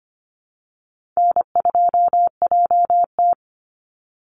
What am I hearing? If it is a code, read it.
Morse code, heard as a keyed tone, about 25 wpm, 700 Hz. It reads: N2JT
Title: Easy Call Signs Repeated 2x - 25wpm